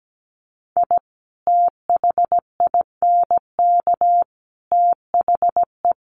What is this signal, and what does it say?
Morse code, heard as a keyed tone, about 17 wpm, 700 Hz. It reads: I THINK THE
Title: Sets of 5 Words Encore 17wpm